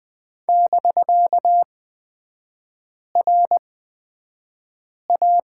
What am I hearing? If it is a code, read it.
Morse code, heard as a keyed tone, about 20 wpm, spaced as for 5 wpm, 700 Hz. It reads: <BK>RA
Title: 250 Lesson 40e - BK - Three Characters 20wpm